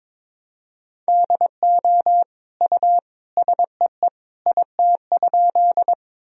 Morse code, heard as a keyed tone, about 22 wpm, 700 Hz. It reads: DO U SEE IT?
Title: Sentences from Top 100 Words and Abbreviations 22wpm